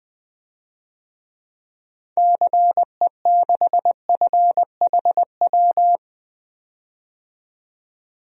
Morse code, heard as a keyed tone, about 20 wpm, 700 Hz. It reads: CE6FHW
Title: DXCC-100 Call Signs - 6 Characters - 20wpm